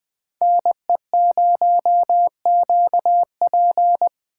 Morse code, heard as a keyed tone, about 20 wpm, 700 Hz. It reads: NE0QP
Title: USA Call Signs - 4x Character Spacing - 20wpm